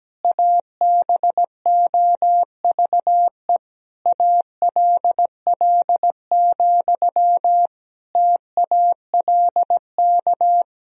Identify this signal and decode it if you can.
Morse code, heard as a keyed tone, about 17 wpm, 700 Hz. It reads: ABOVE ALL, TALK
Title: Sentences from Top 1000 Words - Review Part 2 - 17wpm